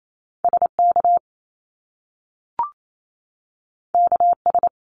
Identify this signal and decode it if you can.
Morse code, heard as a keyed tone, about 28 wpm, 700 Hz. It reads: SK KS
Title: Top Two Letter Combinations in Words 28wpm